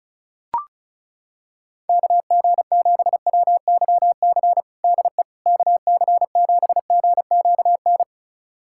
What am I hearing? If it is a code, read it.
Morse code, heard as a keyed tone, about 34 wpm, 700 Hz. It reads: KG7WYC DE KC7GQN
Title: USA Callsign DE Callsign 35wpm